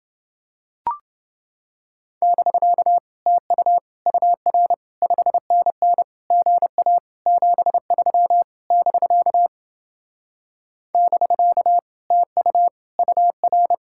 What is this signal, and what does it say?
Morse code, heard as a keyed tone, about 28 wpm, 700 Hz. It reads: <BK> TU UR 5NN GA 73 <BK> <BK> TU UR